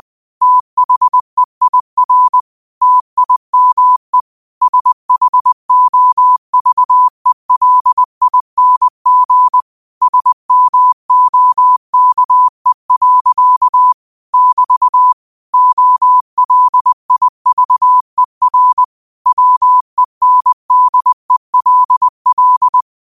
Morse code, heard as a keyed tone, about 20 wpm, 1000 Hz. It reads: THEIR TIME SHOVELING SMOKE. <BT> OLIVER WENDELL
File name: qotdfiles / 20wpm